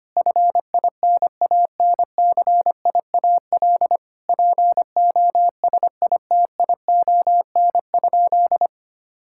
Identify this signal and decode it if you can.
Morse code, heard as a keyed tone, about 25 wpm, 700 Hz. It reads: FINANCIAL POSITION?
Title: Sentences from Top 1000 Words - Review Part 5 - 25wpm